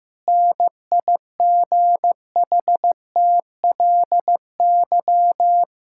Morse code, heard as a keyed tone, about 15 wpm, 700 Hz. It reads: NIGHTLY
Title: Pangram Challenge v2 - 15wpm